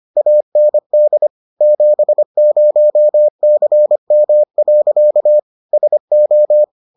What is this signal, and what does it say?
Morse code, heard as a keyed tone, about 25 wpm, 600 Hz. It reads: AND 70CM. SO